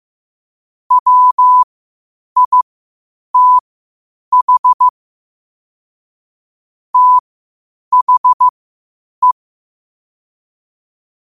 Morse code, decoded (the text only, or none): WITH THE